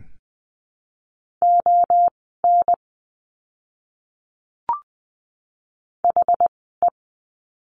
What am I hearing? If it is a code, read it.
Morse code, heard as a keyed tone, about 20 wpm, spaced as for 15 wpm, 700 Hz. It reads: ON HE
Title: 230 Lesson 37f - Letter x - Words 20wpm